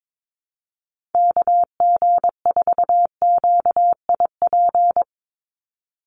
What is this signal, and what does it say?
Morse code, heard as a keyed tone, about 22 wpm, 700 Hz. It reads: KG4QIP
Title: USA Call Signs 22wpm